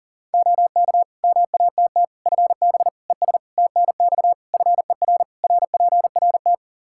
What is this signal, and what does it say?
Morse code, heard as a keyed tone, about 37 wpm, 700 Hz. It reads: OK MATT FB ES TNX FER RPRT
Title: Practice QSO - Exchange 2 - Info - 40wpm